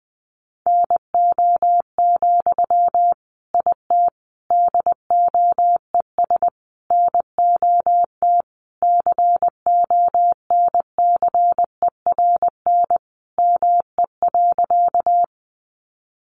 Morse code, heard as a keyed tone, about 20 wpm, 700 Hz. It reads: NO, IT DOES NOT CONCERN ME.
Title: Sentences from Top 1000 Words 20wpm